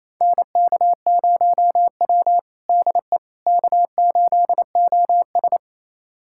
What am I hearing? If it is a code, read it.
Morse code, heard as a keyed tone, about 28 wpm, 700 Hz. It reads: NK0W DE K8OS
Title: USA Callsign DE Callsign - 4 Characters - 28wpm